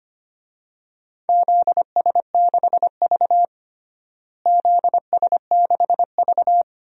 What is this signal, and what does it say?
Morse code, heard as a keyed tone, about 25 wpm, 700 Hz. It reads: ZS6V ZS6V
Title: DXCC-100 Call Signs - 4 Characters - Repeated 2x - 25wpm